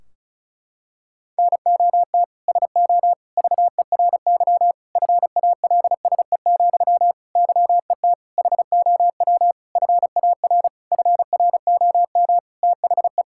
Morse code, heard as a keyed tone, about 34 wpm, 700 Hz. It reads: NOT SO VERY FALSE, YET HOW FAR FROM THE
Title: Sentences from Top 1600 Words 35wpm